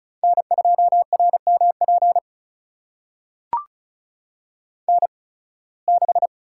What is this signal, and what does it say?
Morse code, heard as a keyed tone, about 34 wpm, spaced as for 23 wpm, 700 Hz. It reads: N2RMP N6
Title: USA Call Signs - 5 Characters - 8x Character Spacing - 35wpm